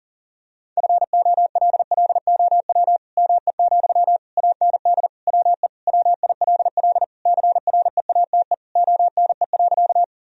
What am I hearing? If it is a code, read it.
Morse code, heard as a keyed tone, about 40 wpm, 700 Hz. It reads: FOLLOW ME, AND WE WILL CREATE ONE.
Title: Sentences from Top 1000 Words - Review Part 3 - 40wpm